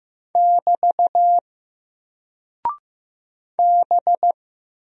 Morse code, heard as a keyed tone, about 15 wpm, 700 Hz. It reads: <BT> B
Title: Single Character (Letter:Number:Punctuation:Prosign) Alt 15wpm